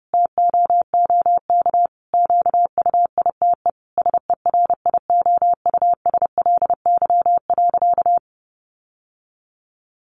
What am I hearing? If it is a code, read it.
Morse code, heard as a keyed tone, about 30 wpm, 700 Hz. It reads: TOOK QUITE SERIOUSLY.
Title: Sentences from Top 1200 Words 30wpm